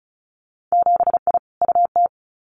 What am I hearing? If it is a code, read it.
Morse code, heard as a keyed tone, about 36 wpm, 700 Hz. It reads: 7I UT
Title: Winter Field Day Exchanges Encore 35wpm